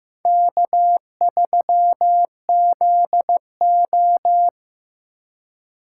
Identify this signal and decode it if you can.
Morse code, heard as a keyed tone, about 15 wpm, 700 Hz. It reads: K3ZO
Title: USA Call Signs - 4 Characters - 4x Character Spacing - 15wpm